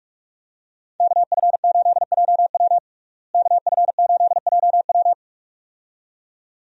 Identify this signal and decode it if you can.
Morse code, heard as a keyed tone, about 46 wpm, 700 Hz. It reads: KF8JW KF8JW
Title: USA Call Signs - 5 Characters - Repeated 2x - 45wpm